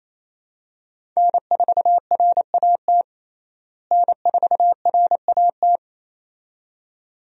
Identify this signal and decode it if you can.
Morse code, heard as a keyed tone, about 27 wpm, 700 Hz. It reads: N4RAT N4RAT
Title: USA Call Signs - 5 Characters - Repeated 2x - 28wpm